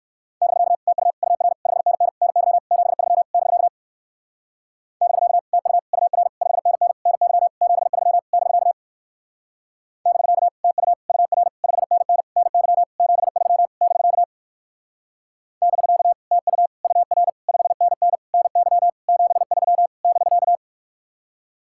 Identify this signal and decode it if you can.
Morse code, heard as a keyed tone, about 57 wpm, 700 Hz. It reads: <BK> TU UR 5NN NY 73 <BK> <BK> TU UR 5NN NY 73 <BK> <BK> TU UR 5NN NY 73 <BK> <BK> TU UR 5NN NY 73 <BK>